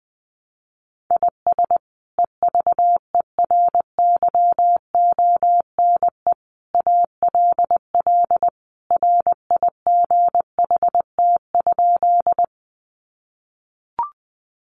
Morse code, heard as a keyed tone, about 20 wpm, 700 Hz. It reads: IS EVERYONE ALL RIGHT?
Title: Sentences from Top 800 Words 20wpm